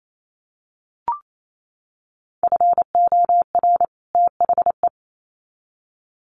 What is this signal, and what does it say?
Morse code, heard as a keyed tone, about 28 wpm, 700 Hz. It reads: FOR THE
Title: Sets of 2 Words 28wpm